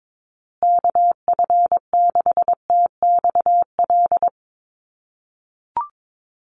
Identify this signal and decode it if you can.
Morse code, heard as a keyed tone, about 22 wpm, 700 Hz. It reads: KF6TXL
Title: USA Call Signs Repeated 3x 22wpm